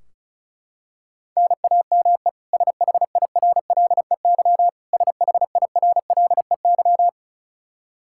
Morse code, heard as a keyed tone, about 36 wpm, 700 Hz. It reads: NAME SHIRLEY SHIRLEY
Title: K1USN SST QSO Practice 35wpm